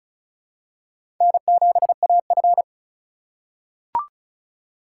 Morse code, heard as a keyed tone, about 36 wpm, 700 Hz. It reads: N7AF
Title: USA Call Signs - 4 Characters - 2x Character Spacing - 35wpm